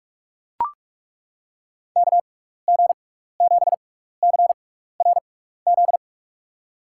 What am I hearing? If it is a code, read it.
Morse code, heard as a keyed tone, about 42 wpm, spaced as for 15 wpm, 700 Hz. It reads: KG7CRZ
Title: USA Call Signs - 6 Characters - 6x Character Spacing - 45wpm